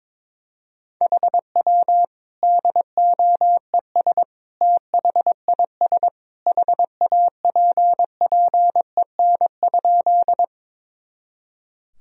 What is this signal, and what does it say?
Morse code, heard as a keyed tone, about 22 wpm, 700 Hz. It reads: HW DOES THIS HAPPEN?
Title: Sentences from Top 400 Words and Abbreviations 22wpm